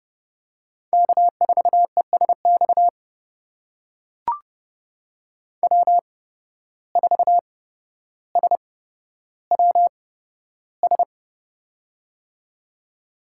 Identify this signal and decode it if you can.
Morse code, heard as a keyed tone, about 30 wpm, spaced as for 13 wpm, 700 Hz. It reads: K4ESX W4SWS
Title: USA Call Signs - 5 Characters - 8x Character Spacing - 30wpm